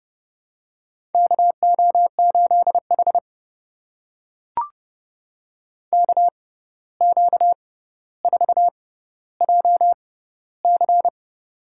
Morse code, heard as a keyed tone, about 30 wpm, spaced as for 14 wpm, 700 Hz. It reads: KO8H KQ4JC
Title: USA Call Signs - 6x Character Spacing - 30wpm